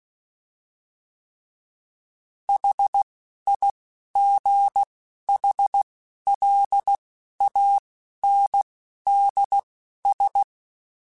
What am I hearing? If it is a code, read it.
Morse code, heard as a keyed tone, about 16 wpm, spaced as for 11 wpm, 800 Hz. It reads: HIGHLANDS